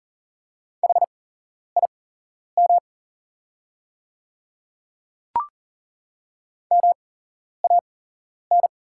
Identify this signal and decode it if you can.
Morse code, heard as a keyed tone, about 40 wpm, spaced as for 11 wpm, 690 Hz. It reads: HIM MAN
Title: 8x Character Spacing - Top 300 Words - 40wpm